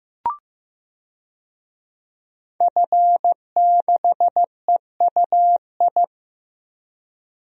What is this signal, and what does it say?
Morse code, heard as a keyed tone, about 15 wpm, 700 Hz. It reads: F6EUI
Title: DXCC-100 Call Signs - 5 Characters - Repeated 2x - 15wpm